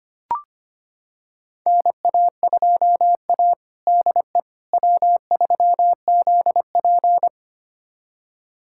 Morse code, heard as a keyed tone, about 25 wpm, 700 Hz. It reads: NA2A DE W3ZP
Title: USA Callsign DE Callsign - 4 Characters - 25wpm